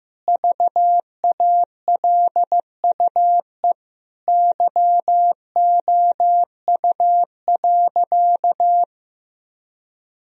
Morse code, heard as a keyed tone, about 15 wpm, 700 Hz. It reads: VALUE YOU.
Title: Sentences from Top 1000 Words - Review Part 3 - 15wpm